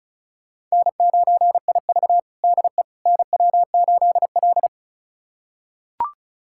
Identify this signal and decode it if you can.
Morse code, heard as a keyed tone, about 36 wpm, 700 Hz. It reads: N9IV DE NW8L
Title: USA Callsign DE Callsign - 4 Characters - 35wpm